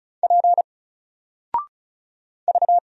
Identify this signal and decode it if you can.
Morse code, heard as a keyed tone, about 35 wpm, 700 Hz. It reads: P V
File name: Single-Letter-Number-35wpm